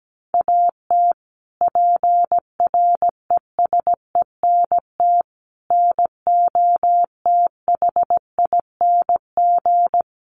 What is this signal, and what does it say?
Morse code, heard as a keyed tone, about 17 wpm, 700 Hz. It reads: AT PRESENT NOTHING
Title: Sentences from Top 1500 Words 17wpm